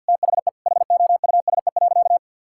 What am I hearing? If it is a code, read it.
Morse code, heard as a keyed tone, about 50 wpm, 700 Hz. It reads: THE HOUSE.
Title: Sentences from Top 1000 Words - Review Part 5 - 50wpm